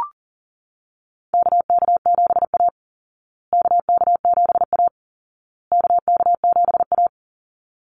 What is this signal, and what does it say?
Morse code, heard as a keyed tone, about 40 wpm, 700 Hz. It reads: KK7A KK7A KK7A
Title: Call Signs Repeated 2x - 40wpm